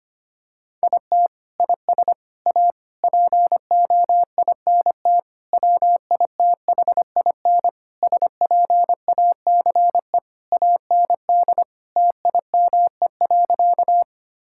Morse code, heard as a keyed tone, about 25 wpm, 700 Hz. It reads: IT IS A POINT WITHIN SPACE AND TIME.
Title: Sentences from Top 1000 Words - Review Part 5 - 25wpm